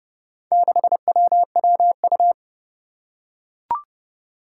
Morse code, heard as a keyed tone, about 30 wpm, 700 Hz. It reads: 6WWU
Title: Four Challenging Letter-Number Combinations 30wpm